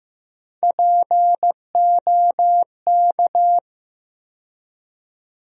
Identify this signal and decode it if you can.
Morse code, heard as a keyed tone, about 16 wpm, 700 Hz. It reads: POK
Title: USA Call Sign Postfixes - 3 Letters 15wpm